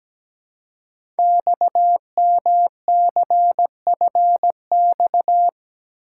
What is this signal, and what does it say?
Morse code, heard as a keyed tone, about 17 wpm, 700 Hz. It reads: XMCFX
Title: Code Groups - 5 Letters - 17wpm